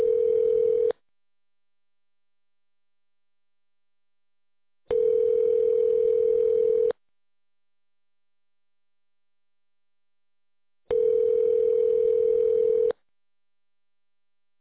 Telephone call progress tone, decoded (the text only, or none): ringback tone